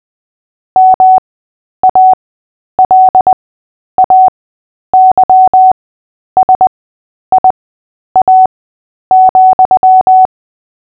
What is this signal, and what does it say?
Morse code, heard as a keyed tone, about 20 wpm, spaced as for 10 wpm, 750 Hz. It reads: MALAYSIA,
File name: cw / fists / Short-Story-3-10wpm